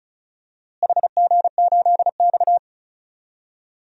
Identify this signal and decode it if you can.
Morse code, heard as a keyed tone, about 34 wpm, 700 Hz. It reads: HG8X
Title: DXCC-100 Call Signs - 4 Characters - Repeated 3x - 35wpm